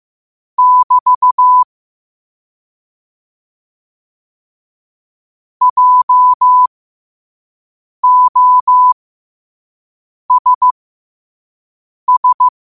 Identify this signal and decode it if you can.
Morse code, heard as a keyed tone, about 15 wpm, spaced as for 5 wpm, 1000 Hz. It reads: <BT> JOSS
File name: 5wpm